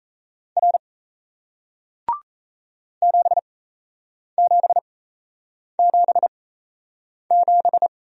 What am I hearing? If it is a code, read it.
Morse code, heard as a keyed tone, about 38 wpm, spaced as for 8 wpm, 700 Hz. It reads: R 7777